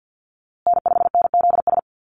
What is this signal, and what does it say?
Morse code, heard as a keyed tone, about 48 wpm, 700 Hz. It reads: N5NZS
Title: USA Call Signs Repeated 3x 50wpm